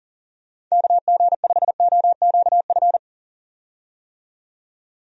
Morse code, heard as a keyed tone, about 40 wpm, 700 Hz. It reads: KG5OQF
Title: USA Call Signs - 6 Characters - 40wpm